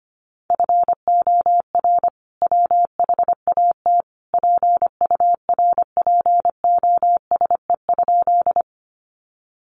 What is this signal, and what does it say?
Morse code, heard as a keyed tone, about 25 wpm, 700 Hz. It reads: FOR WHAT PURPOSE?
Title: Sentences from Top 800 Words 25wpm